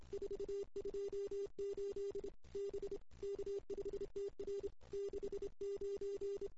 Morse code, heard as a keyed tone, about 25 wpm, 380 Hz. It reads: 428 B K5TR 69
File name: CW / 2008-ss-cw-k5tr-op-n5rz -01:00